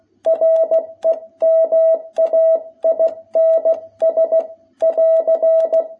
Morse code, heard as a keyed tone, about 16 wpm, 660 Hz. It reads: REMAINS <AR>
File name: jh8cbh / 2021cw